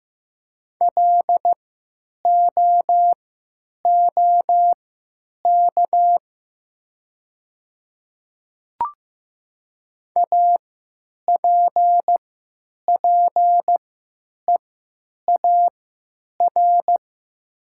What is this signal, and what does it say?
Morse code, heard as a keyed tone, about 15 wpm, spaced as for 9 wpm, 700 Hz. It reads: LOOK APPEAR
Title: 3x Character Spacing - Top 500 Words - 15wpm